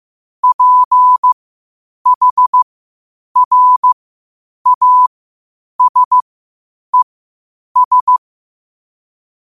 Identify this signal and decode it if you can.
Morse code, heard as a keyed tone, about 15 wpm, spaced as for 8 wpm, 1000 Hz. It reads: PHRASES
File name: qotdfiles / 8wpm